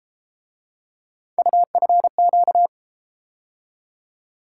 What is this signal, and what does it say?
Morse code, heard as a keyed tone, about 32 wpm, 700 Hz. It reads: UFQ